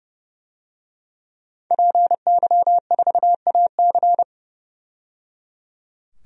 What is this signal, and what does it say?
Morse code, heard as a keyed tone, about 30 wpm, 700 Hz. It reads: PY4AC
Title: DXCC-100 Call Signs - 5 Characters - 30wpm